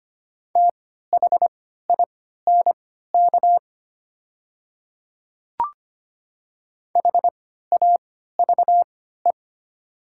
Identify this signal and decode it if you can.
Morse code, heard as a keyed tone, about 25 wpm, spaced as for 14 wpm, 700 Hz. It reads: THINK HAVE
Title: 3x Character Spacing - Top 300 Words - 25wpm